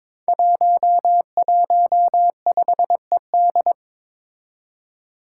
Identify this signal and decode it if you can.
Morse code, heard as a keyed tone, about 22 wpm, 700 Hz. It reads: 115ED
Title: Five Challenging Letter-Number Combinations 22wpm